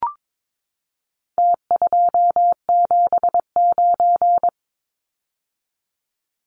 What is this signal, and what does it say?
Morse code, heard as a keyed tone, about 22 wpm, 700 Hz. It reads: T279
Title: Numbers - Contest Style - 4-Digit 22wpm